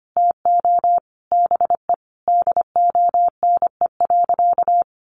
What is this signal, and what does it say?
Morse code, heard as a keyed tone, about 25 wpm, 700 Hz. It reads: TO BE DONE.
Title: Sentences from Top 700 Words 25wpm